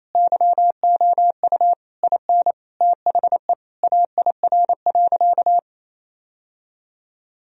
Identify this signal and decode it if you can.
Morse code, heard as a keyed tone, about 28 wpm, 700 Hz. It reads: YOU IN THE AIR.
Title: Sentences from Top 1000 Words - Review Part 1 - 28wpm